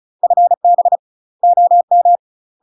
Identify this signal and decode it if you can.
Morse code, heard as a keyed tone, about 36 wpm, 700 Hz. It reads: FB OM
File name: UK MASTER QSOs TRAINING / 35WPM / UK MASTER QSOs 0033